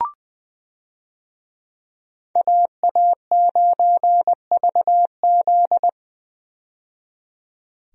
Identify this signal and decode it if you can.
Morse code, heard as a keyed tone, about 20 wpm, 700 Hz. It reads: AA9VZ
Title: USA Call Signs - 5 Characters - 20wpm